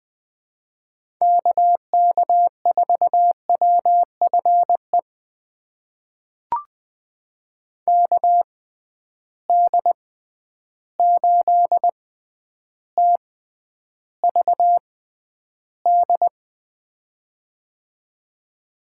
Morse code, heard as a keyed tone, about 20 wpm, spaced as for 10 wpm, 700 Hz. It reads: KK4WFE KD8TVD
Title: USA Call Signs - 6 Characters - 6x Character Spacing - 20wpm